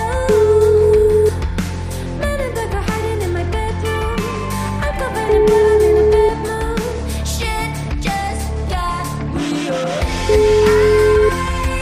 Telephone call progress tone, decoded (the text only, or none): ringback tone